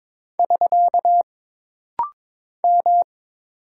Morse code, heard as a keyed tone, about 22 wpm, 700 Hz. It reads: <SK> M
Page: This rapid-fire timing is geared to push your ICR (Instant Character Recognition) skills.